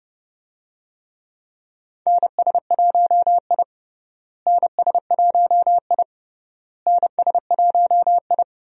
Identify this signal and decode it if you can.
Morse code, heard as a keyed tone, about 30 wpm, 700 Hz. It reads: NS1I NS1I NS1I
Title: USA Call Signs - 4 Characters - Repeated 3x - 30wpm